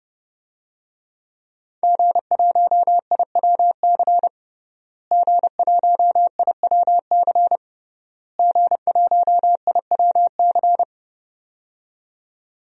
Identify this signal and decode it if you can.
Morse code, heard as a keyed tone, about 30 wpm, 700 Hz. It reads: G1IWC G1IWC G1IWC
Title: DXCC-100 Call Signs - 5 Characters - Repeated 3x - 30wpm